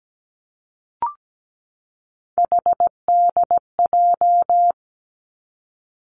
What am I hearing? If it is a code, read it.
Morse code, heard as a keyed tone, about 17 wpm, 700 Hz. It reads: HDJ
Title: Three Challenging Letter-Number Combinations 17wpm